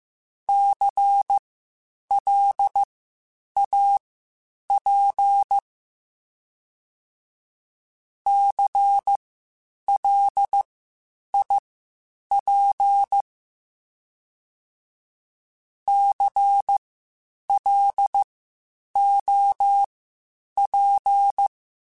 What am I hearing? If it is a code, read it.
Morse code, heard as a keyed tone, about 15 wpm, spaced as for 8 wpm, 800 Hz. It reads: CLAP CLIP CLOP
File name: Spaced Alphas / Spaced Alphas mp3s / Cs sp @13